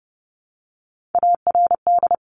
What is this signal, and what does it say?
Morse code, heard as a keyed tone, about 30 wpm, 700 Hz. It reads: ARD
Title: Top Three Letter Combinations in Words 30wpm